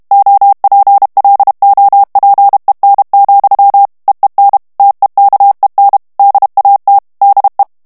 Morse code, heard as a keyed tone, about 32 wpm, 800 Hz. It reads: OPLOPEN, EEN TEKEN DAT DE
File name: HOLLANDAIS texte clair 2 vitesse 30 mots minute